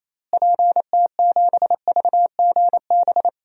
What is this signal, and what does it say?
Morse code, heard as a keyed tone, about 28 wpm, 700 Hz. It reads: PT7VGB
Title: DXCC-100 Call Signs - 6 Characters - Repeated 2x - 28wpm